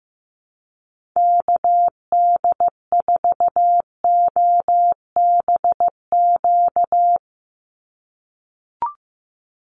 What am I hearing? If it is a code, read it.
Morse code, heard as a keyed tone, about 15 wpm, 700 Hz. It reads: KD4OBQ
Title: USA Call Signs 15wpm